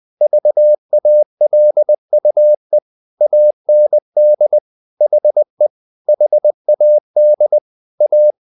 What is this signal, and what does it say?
Morse code, heard as a keyed tone, about 20 wpm, 600 Hz. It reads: VALUE AND HE HAD A
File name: the old man and the sea / the old man and the sea part 1 20wpm